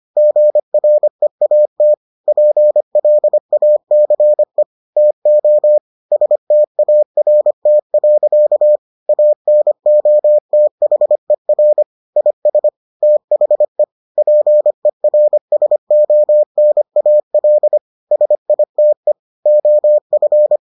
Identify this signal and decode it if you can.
Morse code, heard as a keyed tone, about 25 wpm, 600 Hz. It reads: GREAT PLACE TO START. ANOTHER IS THE PERSONAL SITE OF